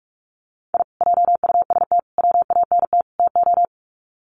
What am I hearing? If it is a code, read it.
Morse code, heard as a keyed tone, about 44 wpm, 700 Hz. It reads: I JUST WANT TO
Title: Sets of 4 Words 45wpm